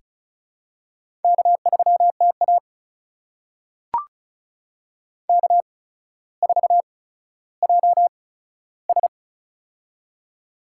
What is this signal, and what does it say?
Morse code, heard as a keyed tone, about 36 wpm, spaced as for 15 wpm, 700 Hz. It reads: K3TA K4JS
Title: USA Call Signs - 4 Characters - 8x Character Spacing - 35wpm